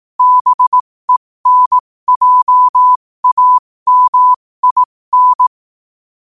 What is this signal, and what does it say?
Morse code, heard as a keyed tone, about 18 wpm, spaced as for 15 wpm, 1000 Hz. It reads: BENJAMIN
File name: qotdfiles / 15wpm